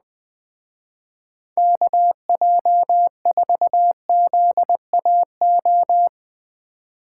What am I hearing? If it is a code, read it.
Morse code, heard as a keyed tone, about 20 wpm, 700 Hz. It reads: KJ4ZAO
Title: Speed Racing - USA 2x3 Call Signs 17wpm